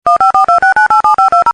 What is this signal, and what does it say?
DTMF, heard as telephone touch tones, tones of about 120 ms. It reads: 15426987520